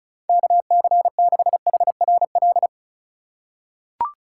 Morse code, heard as a keyed tone, about 36 wpm, 700 Hz. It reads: KC6HRL
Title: USA Call Signs - 6 Characters - 4x Character Spacing - 35wpm